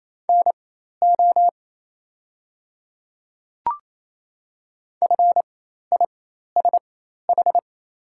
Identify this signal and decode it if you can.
Morse code, heard as a keyed tone, about 28 wpm, spaced as for 13 wpm, 700 Hz. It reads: NO FISH
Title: 4x Character Spacing - Top 300 Words - 28wpm